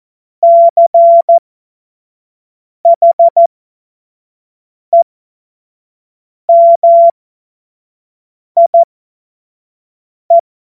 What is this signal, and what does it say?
Morse code, heard as a keyed tone, about 14 wpm, spaced as for 5 wpm, 680 Hz. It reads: CHEMIE
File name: Morsekurs / Witze / Witz 28. September 2021, 25 F70